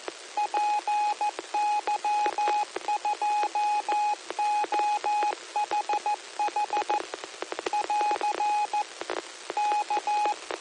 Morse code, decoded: PY2OHH <AR> K

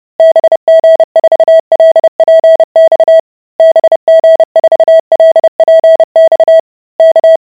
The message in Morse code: BG4LPX BG4LPX K

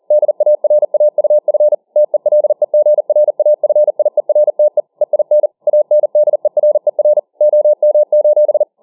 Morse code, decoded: DARAUF TELEGRAFIERTE EIN ANDERER OM:

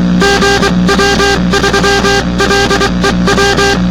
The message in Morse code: GW3LEW

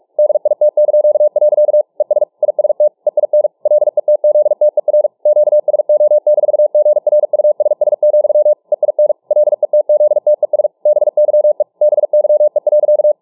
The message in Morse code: DIT). ES IST EIN LETZTER QSO-GRUSS, EIN LETZTES BYE BYE.

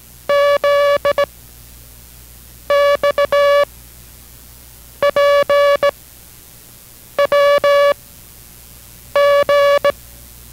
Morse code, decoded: Z X P W G

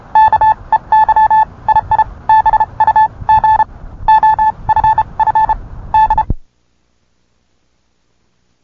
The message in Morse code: KEYII BUG OFF D

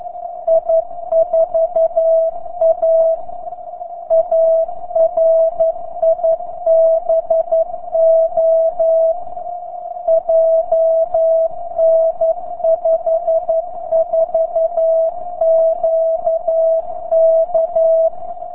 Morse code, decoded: I4AA RIBO JN54QK

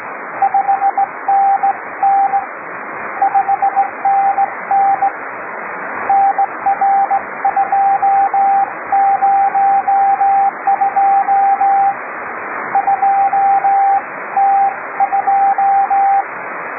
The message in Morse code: 5NN 5NN NR202 2T2

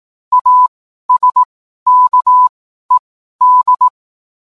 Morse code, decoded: ASKED

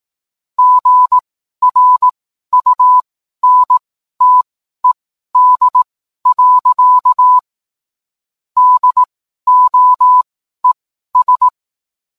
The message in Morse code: GRUNTED. DOES

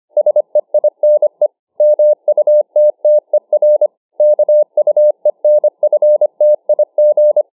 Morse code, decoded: SEINE MUTTER KUENFTIG

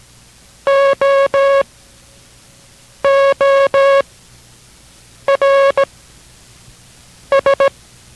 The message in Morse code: OORS

